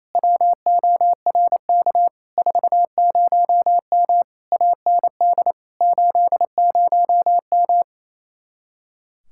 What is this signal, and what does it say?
Morse code, heard as a keyed tone, about 27 wpm, 700 Hz. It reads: WORK 40M AND 80M
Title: Speed-Racing Common QSO Phrases Encore - 28wpm